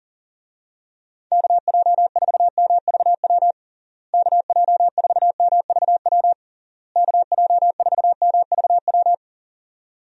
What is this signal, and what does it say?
Morse code, heard as a keyed tone, about 40 wpm, 700 Hz. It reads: KJ4MVW KJ4MVW KJ4MVW
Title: USA Call Signs - 6 Characters - Repeated 3x - 40wpm